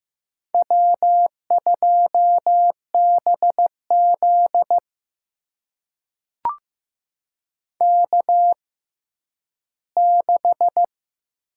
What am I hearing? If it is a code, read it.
Morse code, heard as a keyed tone, about 15 wpm, spaced as for 10 wpm, 700 Hz. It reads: W2BZ K6
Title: USA Call Signs - 4 Characters - 6x Character Spacing - 15wpm